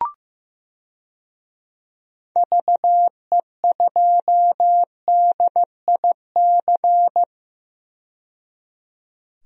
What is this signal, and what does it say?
Morse code, heard as a keyed tone, about 15 wpm, 700 Hz. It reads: VE2DIC
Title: DXCC-100 Call Signs - 6 Characters - 15wpm